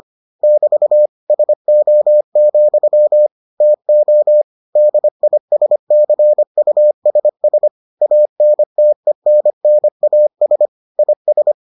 Morse code, heard as a keyed tone, about 25 wpm, 600 Hz. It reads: <BT> SO, TO DISCUSS ANTENNAS IS